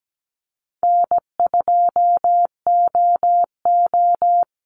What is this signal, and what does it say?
Morse code, heard as a keyed tone, about 17 wpm, 700 Hz. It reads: N2OO
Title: Easy Call Signs Repeated 2x - 17wpm